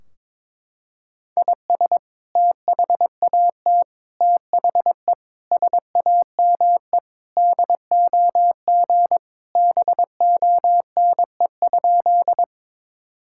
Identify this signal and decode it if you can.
Morse code, heard as a keyed tone, about 22 wpm, 700 Hz. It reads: IS THAT THE SAME DOG BONE?
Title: Sentences from Top 1600 Words 22wpm